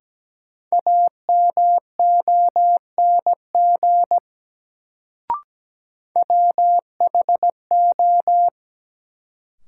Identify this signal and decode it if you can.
Morse code, heard as a keyed tone, about 17 wpm, 700 Hz. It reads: AMONG WHO
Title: Sets of 2 Words - New Word Drill - 17wpm